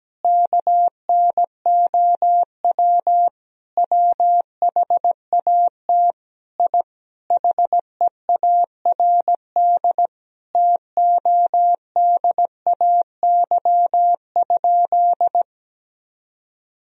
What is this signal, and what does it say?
Morse code, heard as a keyed tone, about 17 wpm, 700 Hz. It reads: KNOW WHAT I HEARD TODAY?
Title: Sentences from Top 1000 Words - Review Part 4 - 17wpm